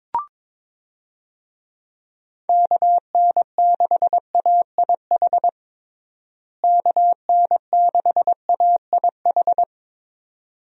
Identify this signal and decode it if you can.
Morse code, heard as a keyed tone, about 22 wpm, 700 Hz. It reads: KN6AIH KN6AIH
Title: USA Call Signs - 6 Characters - Repeated 2x - 22wpm